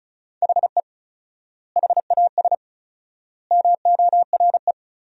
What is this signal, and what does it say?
Morse code, heard as a keyed tone, about 34 wpm, 700 Hz. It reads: HE HAS MORE
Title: 4x Word-Spacing - Sentences from Top 200 Words - 35wpm